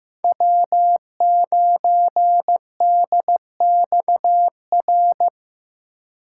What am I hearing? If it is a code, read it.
Morse code, heard as a keyed tone, about 15 wpm, 700 Hz. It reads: W9DXR
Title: K1USN SST QSO Practice - 15wpm